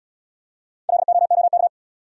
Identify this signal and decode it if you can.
Morse code, heard as a keyed tone, about 68 wpm, 700 Hz. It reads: LOOK